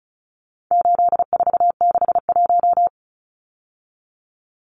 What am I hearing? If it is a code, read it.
Morse code, heard as a keyed tone, about 34 wpm, 700 Hz. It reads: 8461
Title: Numbers - 4-Digit 35wpm